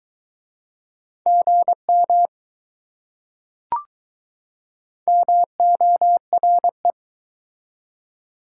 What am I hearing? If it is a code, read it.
Morse code, heard as a keyed tone, about 21 wpm, 700 Hz. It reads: GM MORE